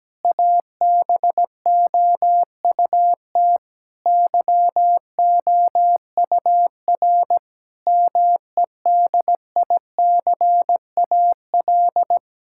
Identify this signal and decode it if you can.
Morse code, heard as a keyed tone, about 17 wpm, 700 Hz. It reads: ABOUT YOUR MEDICAL
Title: Sentences from Top 1000 Words - Review Part 5 - 17wpm